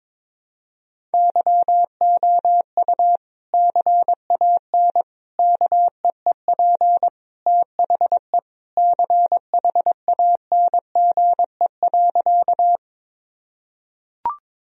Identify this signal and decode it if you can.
Morse code, heard as a keyed tone, about 22 wpm, 700 Hz. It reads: YOU CAN KEEP THE CHANGE.
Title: Sentences from Top 1000 Words - Review Part 1 - 22wpm